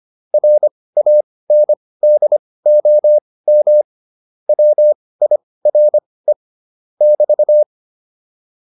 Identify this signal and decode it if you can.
Morse code, heard as a keyed tone, about 25 wpm, spaced as for 18 wpm, 600 Hz. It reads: RANDOM WIRE <BT>